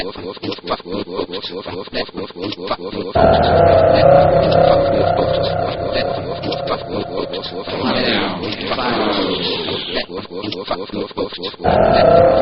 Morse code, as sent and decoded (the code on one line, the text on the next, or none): none